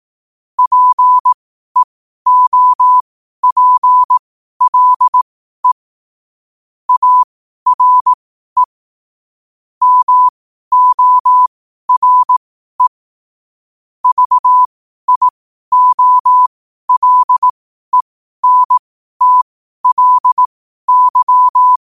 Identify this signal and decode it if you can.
Morse code, heard as a keyed tone, about 18 wpm, spaced as for 12 wpm, 1000 Hz. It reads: PEOPLE ARE MORE VIOLENTLY
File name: qotdfiles / 12wpm